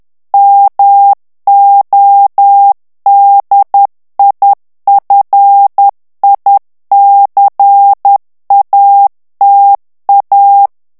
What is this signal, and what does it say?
Morse code, heard as a keyed tone, about 10 wpm, 800 Hz. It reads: MODIFICATA
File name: ITALIEN texte clair 1 vitesse 10 mots minute